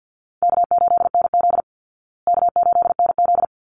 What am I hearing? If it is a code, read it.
Morse code, heard as a keyed tone, about 50 wpm, 700 Hz. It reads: K8NZ K8NZ
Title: Call Signs Repeated 3x - 50wpm